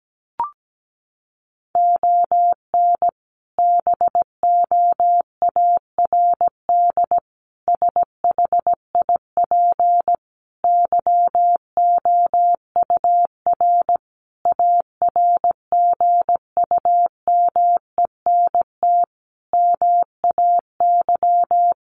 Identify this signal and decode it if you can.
Morse code, heard as a keyed tone, about 17 wpm, 700 Hz. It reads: ON BOARD SHIP YOUR ARGUMENT MAY
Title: Sentences from Top 1400 Words 17wpm